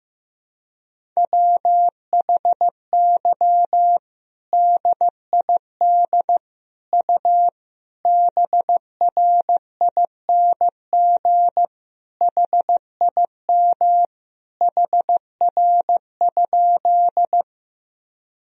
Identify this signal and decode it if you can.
Morse code, heard as a keyed tone, about 15 wpm, 700 Hz. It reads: WHY DID U BRING HIM HR?
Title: Sentences from Top 500 Words and Abbreviations 15wpm